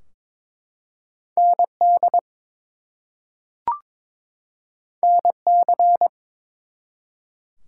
Morse code, heard as a keyed tone, about 22 wpm, 700 Hz. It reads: ND NC
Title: US States and Canadian Provinces 22wpm